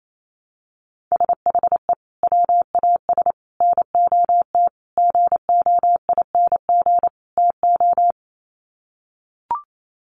Morse code, heard as a keyed tone, about 28 wpm, 700 Hz. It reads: SHE WAS NOT GOING TO
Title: Sets of 5 Words Encore2 28wpm